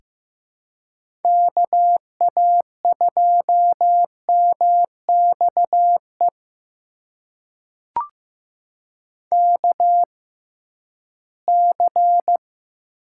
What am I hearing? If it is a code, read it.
Morse code, heard as a keyed tone, about 15 wpm, spaced as for 11 wpm, 700 Hz. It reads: KA2MXE KC
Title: USA Call Signs - 6 Characters - 6x Character Spacing - 15wpm